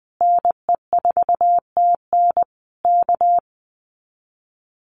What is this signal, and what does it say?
Morse code, heard as a keyed tone, about 20 wpm, 700 Hz. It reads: NE4TN K
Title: Common QSO Phrases Encore 20wpm